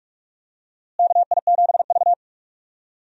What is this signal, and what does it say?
Morse code, heard as a keyed tone, about 43 wpm, 700 Hz. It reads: KI7V
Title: USA Call Signs - 4 Characters - Repeated 2x - 45wpm